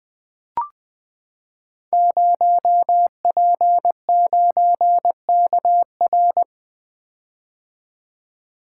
Morse code, heard as a keyed tone, about 20 wpm, 700 Hz. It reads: 0P9KR
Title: Code Groups - 5 Letter:Numbers - 20wpm